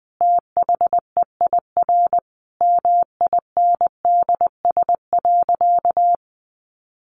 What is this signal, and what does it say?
Morse code, heard as a keyed tone, about 20 wpm, 700 Hz. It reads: THEIR MINDS.
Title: Easy Phrases 20wpm